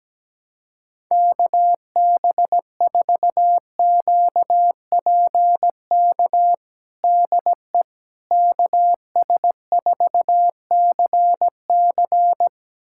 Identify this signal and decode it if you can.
Morse code, heard as a keyed tone, about 17 wpm, 700 Hz. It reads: KB4QPK DE KS4CC
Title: USA Callsign DE Callsign 17wpm